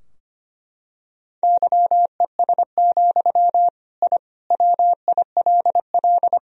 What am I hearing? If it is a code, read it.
Morse code, heard as a keyed tone, about 25 wpm, 700 Hz. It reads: YES, I WILL
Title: Sentences from Top 1000 Words - Review Part 2 - 25wpm